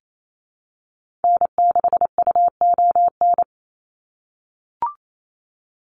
Morse code, heard as a keyed tone, about 28 wpm, 700 Hz. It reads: N6UON
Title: USA Call Signs 28wpm